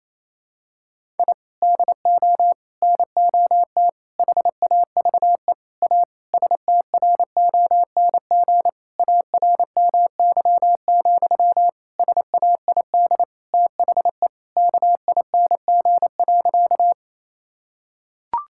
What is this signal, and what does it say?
Morse code, heard as a keyed tone, about 28 wpm, 700 Hz. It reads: I DO NOT HAVE A STRONG ARMY, SAID THE KING.
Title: Sentences from Top 1000 Words - Review Part 5 - 28wpm